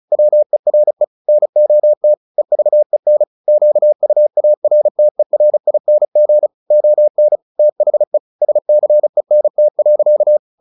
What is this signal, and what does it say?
Morse code, heard as a keyed tone, about 35 wpm, 600 Hz. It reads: WERE NOT EVEN QUARTERING ON THE SCENT.
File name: the old man and the sea 2 35wpm